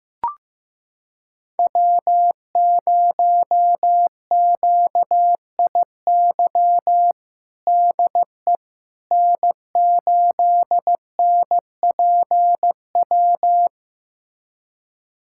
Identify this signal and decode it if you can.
Morse code, heard as a keyed tone, about 15 wpm, 700 Hz. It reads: W0QIY DE N8NPW
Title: USA Callsign DE Callsign - 5 Characters - 15wpm